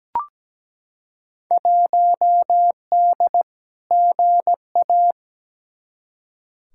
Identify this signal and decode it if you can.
Morse code, heard as a keyed tone, about 17 wpm, 700 Hz. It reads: 1D GA
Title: Field Day Exchanges - 17wpm